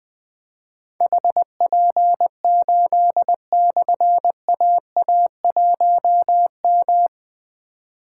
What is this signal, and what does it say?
Morse code, heard as a keyed tone, about 21 wpm, 700 Hz. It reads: HP8/AA1M
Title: Difficult Call Signs 20wpm